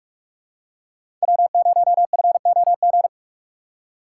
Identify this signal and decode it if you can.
Morse code, heard as a keyed tone, about 46 wpm, 700 Hz. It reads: W0FOG
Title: USA Call Signs - 5 Characters - 3x Character Spacing - 45wpm